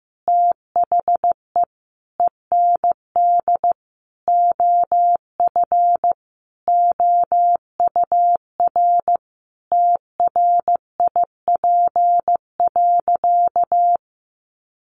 Morse code, heard as a keyed tone, about 14 wpm, 700 Hz. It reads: THE END OF OUR TRIP.
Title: Sentences from Top 1300 Words 15wpm